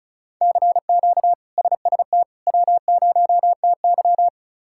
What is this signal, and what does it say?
Morse code, heard as a keyed tone, about 36 wpm, 700 Hz. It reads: CQ SST W0TY
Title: K1USN SST QSO Practice - 35wpm